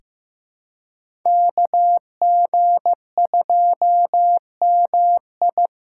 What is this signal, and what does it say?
Morse code, heard as a keyed tone, about 15 wpm, 700 Hz. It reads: KG2MI
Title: USA Callsign DE Callsign 15wpm